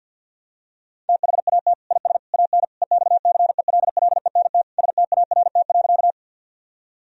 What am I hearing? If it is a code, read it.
Morse code, heard as a keyed tone, about 51 wpm, 700 Hz. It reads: THAT IS AN EXCELLENT START.
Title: Sentences from Top 1000 Words - Review Part 1 - 50wpm